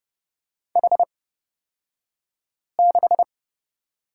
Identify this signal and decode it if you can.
Morse code, heard as a keyed tone, about 30 wpm, spaced as for 5 wpm, 700 Hz. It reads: H6
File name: Lesson 37d - Letter x - Two Characters 30wpm